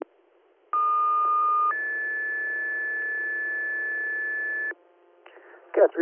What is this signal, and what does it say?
Two-tone page, heard as a one-second tone, then a three-second tone, about 1220 then 1750 Hz, with speech. It is heard from 0.7 to 4.7 s.